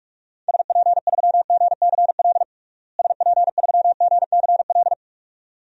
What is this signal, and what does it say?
Morse code, heard as a keyed tone, about 45 wpm, 700 Hz. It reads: SP3GCL SP3GCL
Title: DXCC-100 Call Signs - 6 Characters - Repeated 2x - 45wpm